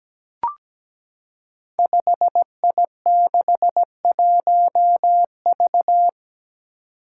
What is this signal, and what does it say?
Morse code, heard as a keyed tone, about 17 wpm, 700 Hz. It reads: 5I61V
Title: Five Challenging Letter-Number Combinations 17wpm